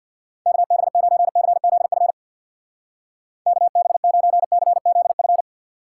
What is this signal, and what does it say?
Morse code, heard as a keyed tone, about 58 wpm, 700 Hz. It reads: KB9CZF KB9CZF
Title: Speed Racing - USA 2x3 Call Signs 50wpm